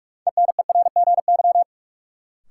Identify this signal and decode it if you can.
Morse code, heard as a keyed tone, about 44 wpm, 700 Hz. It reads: ENERGY
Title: Sentences from Top 900 Words - New Word Drill - 45wpm